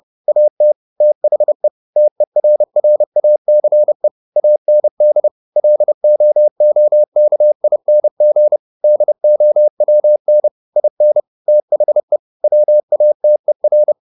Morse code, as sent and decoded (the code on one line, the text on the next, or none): .- - / - .... / - . .-. .-. .- -.-. / .- -. -.. / .-.. --- --- -.- .. -. --. / -.. --- .-- -. / .. -. / - .... / .-- .- - . .-.
AT THE TERRACE AND LOOKING DOWN IN THE WATER